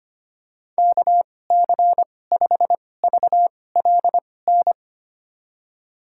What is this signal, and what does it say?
Morse code, heard as a keyed tone, about 25 wpm, spaced as for 18 wpm, 700 Hz. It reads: KC5VLN
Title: USA Call Signs - 2x Character Spacing - 25wpm